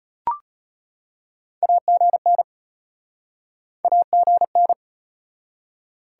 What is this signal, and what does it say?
Morse code, heard as a keyed tone, about 33 wpm, 700 Hz. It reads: AGN AGN